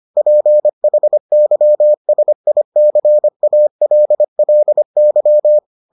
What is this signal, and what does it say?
Morse code, heard as a keyed tone, about 25 wpm, 600 Hz. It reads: PHYSICALLY